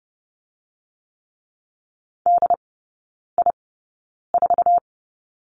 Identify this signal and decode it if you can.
Morse code, heard as a keyed tone, about 30 wpm, spaced as for 9 wpm, 700 Hz. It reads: DI4
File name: course / 30:10wpm / 087 Lesson 15e - Number 5 - Three Characters 30wpm